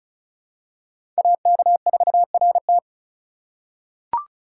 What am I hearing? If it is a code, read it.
Morse code, heard as a keyed tone, about 36 wpm, 700 Hz. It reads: AK4RT
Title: USA Call Signs - 5 Characters - 4x Character Spacing - 35wpm